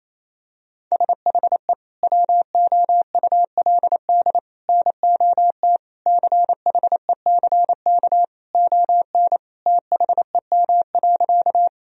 Morse code, heard as a keyed tone, about 28 wpm, 700 Hz. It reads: SHE WOULD NOT CHECK ON THEM.
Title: Sentences from Top 1000 Words - Review Part 3 - 28wpm